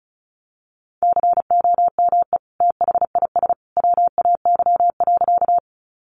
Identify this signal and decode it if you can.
Morse code, heard as a keyed tone, about 34 wpm, 700 Hz. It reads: COME THIS WAY.
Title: Sentences from Top 100 Words 35wpm